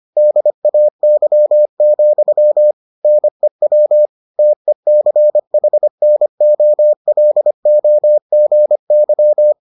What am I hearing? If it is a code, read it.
Morse code, heard as a keyed tone, about 25 wpm, 600 Hz. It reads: DAY, NEW TECHNOLOGY